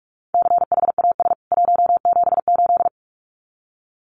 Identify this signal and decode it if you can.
Morse code, heard as a keyed tone, about 44 wpm, 700 Hz. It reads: CHAS 178
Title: CWT Exchange 45wpm